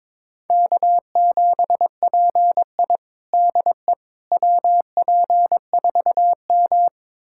Morse code, heard as a keyed tone, about 22 wpm, 700 Hz. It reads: K7PI DE WP4M
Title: USA Callsign DE Callsign - 4 Characters - 22wpm